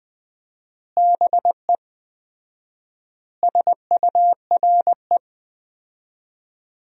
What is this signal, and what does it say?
Morse code, heard as a keyed tone, about 20 wpm, 700 Hz. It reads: BE SURE
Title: 4x Word-Spacing - Sentences from Top 300 Words - 20wpm